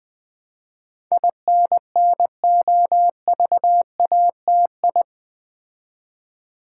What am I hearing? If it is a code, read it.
Morse code, heard as a keyed tone, about 20 wpm, 700 Hz. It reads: INNOVATI
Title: Build A Word - 4 Letters At A Time - 10-Letter Words - 20wpm